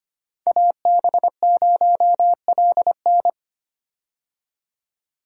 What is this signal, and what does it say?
Morse code, heard as a keyed tone, about 25 wpm, 700 Hz. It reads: AB0LN
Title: USA Call Signs - 5 Characters - 25wpm